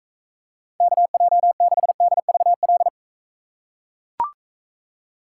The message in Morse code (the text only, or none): KJ6DVL